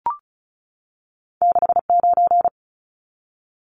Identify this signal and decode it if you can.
Morse code, heard as a keyed tone, about 36 wpm, 700 Hz. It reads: 69
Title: Numbers - 2-Digit 35wpm